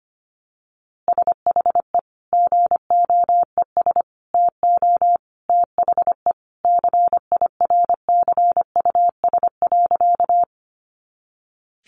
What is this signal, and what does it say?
Morse code, heard as a keyed tone, about 25 wpm, 700 Hz. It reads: SHE GOES TO THE CIRCUS.
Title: Easy Phrases 25wpm